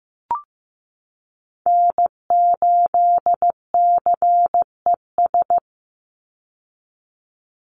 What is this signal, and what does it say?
Morse code, heard as a keyed tone, about 15 wpm, 700 Hz. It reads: N8CES
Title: USA Call Signs 15wpm